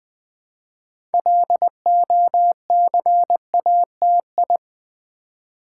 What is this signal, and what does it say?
Morse code, heard as a keyed tone, about 20 wpm, 700 Hz. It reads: LOCATI
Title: Build A Word - 3 Letters At A Time - 8-Letter Words - 20wpm